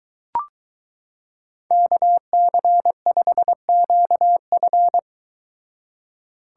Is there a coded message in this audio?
Yes: Morse code from 1.7 to 5.0 s, heard as a keyed tone, about 23 wpm, 700 Hz. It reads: KC5QF